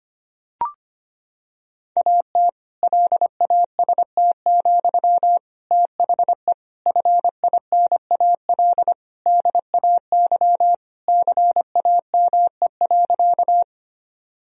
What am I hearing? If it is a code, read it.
Morse code, heard as a keyed tone, about 25 wpm, 700 Hz. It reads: AT LAST, THE FINAL DAY CAME.
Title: Sentences from Top 1000 Words - Review Part 2 - 25wpm